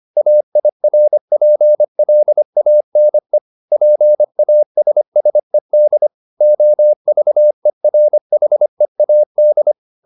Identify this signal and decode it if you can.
Morse code, heard as a keyed tone, about 25 wpm, 600 Hz. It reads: AIRPLANE PASSED OVERHEAD